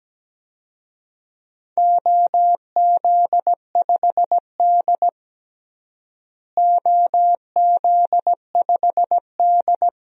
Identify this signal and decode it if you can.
Morse code, heard as a keyed tone, about 17 wpm, 700 Hz. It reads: OZ5D OZ5D
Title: DXCC-100 Call Signs - 4 Characters - Repeated 2x - 17wpm